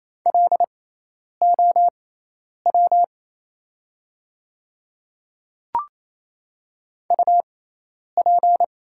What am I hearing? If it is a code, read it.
Morse code, heard as a keyed tone, about 28 wpm, spaced as for 10 wpm, 700 Hz. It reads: LOW UP
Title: 6x Character Spacing - Top 200 Words - 28wpm